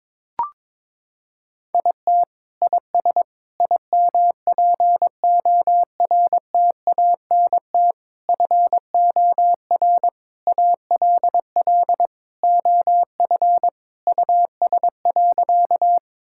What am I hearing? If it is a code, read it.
Morse code, heard as a keyed tone, about 22 wpm, 700 Hz. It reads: IT IS IMPORTANT FOR ALL OF US.
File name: practice / 22wpm / Sentences with 4-Word Phrases - 22wpm